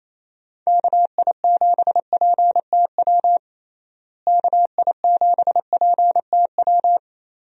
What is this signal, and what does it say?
Morse code, heard as a keyed tone, about 28 wpm, 700 Hz. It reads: KI7PTW KI7PTW
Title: USA Call Signs - 6 Characters - Repeated 2x - 28wpm